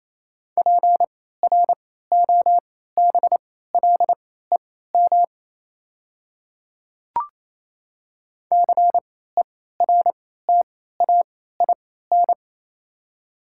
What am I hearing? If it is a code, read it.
Morse code, heard as a keyed tone, about 28 wpm, spaced as for 16 wpm, 700 Hz. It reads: PROBLEM CERTAIN
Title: 3x Character Spacing - Top 500 Words - 28wpm